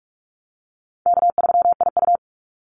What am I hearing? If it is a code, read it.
Morse code, heard as a keyed tone, about 44 wpm, 700 Hz. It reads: K3IU
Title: Easy Call Signs Repeated 2x - 45wpm